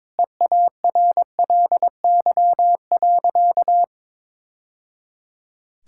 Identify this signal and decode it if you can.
Morse code, heard as a keyed tone, about 22 wpm, 700 Hz. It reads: EARLY.
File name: practice / 22wpm / Sentences with 3-Word Phrases - 22wpm